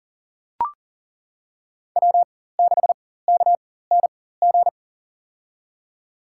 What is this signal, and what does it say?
Morse code, heard as a keyed tone, about 40 wpm, spaced as for 19 wpm, 700 Hz. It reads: W6KNG
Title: USA Call Signs - 5 Characters - 4x Character Spacing - 40wpm